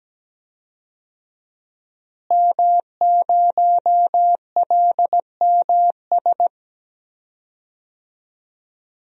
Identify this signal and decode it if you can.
Morse code, heard as a keyed tone, about 17 wpm, 700 Hz. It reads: M0LMS
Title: DXCC-100 Call Signs - 5 Characters - Repeated 2x - 17wpm